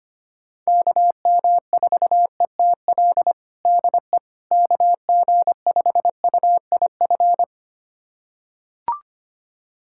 Morse code, heard as a keyed tone, about 25 wpm, 700 Hz. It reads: KM4ETL DE KG5UIF
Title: USA Callsign DE Callsign - 6 Characters - 25wpm